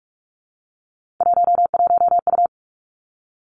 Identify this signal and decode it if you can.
Morse code, heard as a keyed tone, about 46 wpm, 700 Hz. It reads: 11U
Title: Three Challenging Letter-Number Combinations 45wpm